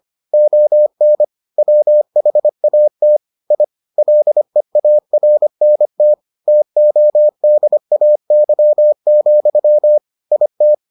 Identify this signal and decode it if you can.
Morse code, heard as a keyed tone, about 25 wpm, 600 Hz. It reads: ON WHAT I LEARNT TODAY, IT